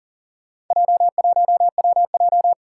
Morse code, heard as a keyed tone, about 41 wpm, 700 Hz. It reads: J1WJ
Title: Four Challenging Letter-Number Combinations 40wpm